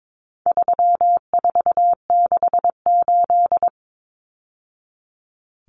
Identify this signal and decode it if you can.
Morse code, heard as a keyed tone, about 22 wpm, 700 Hz. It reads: 3468
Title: Numbers - 4-Digit 22wpm